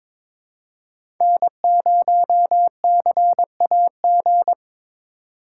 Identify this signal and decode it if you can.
Morse code, heard as a keyed tone, about 22 wpm, 700 Hz. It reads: N0CAG
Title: USA Call Signs - 5 Characters - 3x Character Spacing - 22wpm